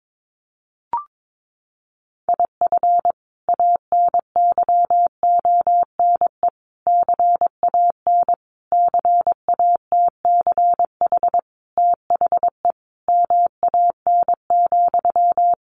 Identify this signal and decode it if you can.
Morse code, heard as a keyed tone, about 22 wpm, 700 Hz. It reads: IF ANYONE CAN CATCH THE MAN,
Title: Sentences from Top 1500 Words 22wpm